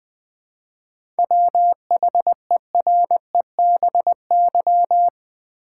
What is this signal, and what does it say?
Morse code, heard as a keyed tone, about 20 wpm, 700 Hz. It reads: WHEREBY
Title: 7 Letter Words - First 3 Letters Given - 20wpm